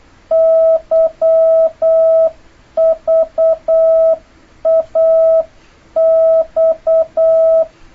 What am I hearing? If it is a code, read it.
Morse code, heard as a keyed tone, about 8 wpm, 650 Hz. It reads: YVAX
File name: jh8cbh / 2026cw